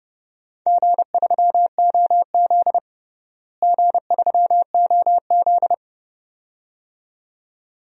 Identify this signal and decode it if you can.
Morse code, heard as a keyed tone, about 30 wpm, 700 Hz. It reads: G3OZ G3OZ
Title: DXCC-100 Call Signs - 4 Characters - Repeated 2x - 30wpm